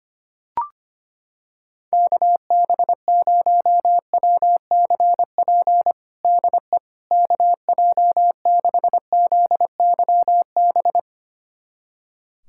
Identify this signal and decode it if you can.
Morse code, heard as a keyed tone, about 25 wpm, 700 Hz. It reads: KB0WCP DE KJ6ZYB
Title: USA Callsign DE Callsign - 6 Characters - 25wpm